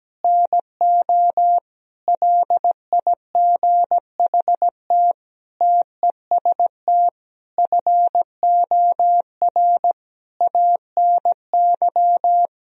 Morse code, heard as a keyed tone, about 17 wpm, 700 Hz. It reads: NO LIGHT TEST FOR ANY
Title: Sentences from Top 1000 Words - Review Part 3 - 17wpm